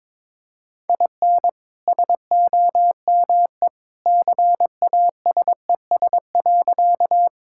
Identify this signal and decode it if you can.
Morse code, heard as a keyed tone, about 22 wpm, 700 Hz. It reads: IN SOME CASES.
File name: practice / 22wpm / Sentences with 2-Word Phrases - 22wpm